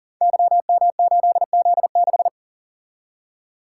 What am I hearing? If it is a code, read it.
Morse code, heard as a keyed tone, about 40 wpm, 690 Hz. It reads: YM8Z6
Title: Code Groups - 5 Letter:Numbers - 40wpm